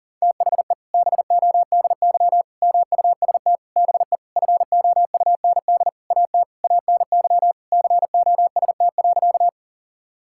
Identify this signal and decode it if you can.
Morse code, heard as a keyed tone, about 40 wpm, 700 Hz. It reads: THE BODY MUST BE FOUND AT ANY COST.
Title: Sentences from Top 1000 Words - Review Part 5 - 40wpm